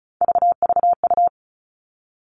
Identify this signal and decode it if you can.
Morse code, heard as a keyed tone, about 34 wpm, 700 Hz. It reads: VVU
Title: All Cut Numbers - 3-Digit 35wpm